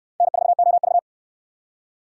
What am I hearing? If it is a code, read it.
Morse code, heard as a keyed tone, about 69 wpm, 700 Hz. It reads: N4OV